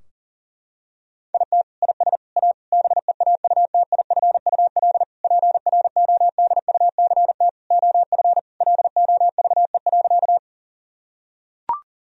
Morse code, heard as a keyed tone, about 40 wpm, 700 Hz. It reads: IT IS A BEAUTIFUL PRODUCT OF LOVE.
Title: Sentences from Top 1000 Words - Review Part 2 - 40wpm